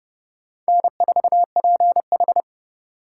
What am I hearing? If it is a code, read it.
Morse code, heard as a keyed tone, about 30 wpm, 700 Hz. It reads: N4PH
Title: USA Call Signs - 4 Characters - 3x Character Spacing - 30wpm